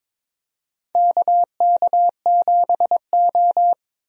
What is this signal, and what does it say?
Morse code, heard as a keyed tone, about 22 wpm, 700 Hz. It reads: KK7O
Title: USA Call Signs - 4 Characters - Repeated 3x - 22wpm